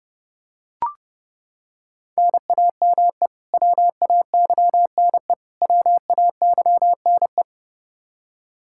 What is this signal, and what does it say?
Morse code, heard as a keyed tone, about 30 wpm, 700 Hz. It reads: NAME WAYNE WAYNE
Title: Practice QSO - Exchange 1 - Beginning - 30wpm